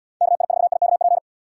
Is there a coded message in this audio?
Yes: Morse code from 0.2 to 1.2 s, heard as a keyed tone, about 70 wpm, 700 Hz. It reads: KI4IKP